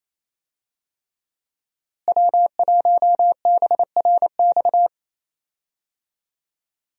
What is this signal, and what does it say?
Morse code, heard as a keyed tone, about 28 wpm, 700 Hz. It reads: W1BRX
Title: USA Call Signs - 5 Characters - Repeated 3x - 28wpm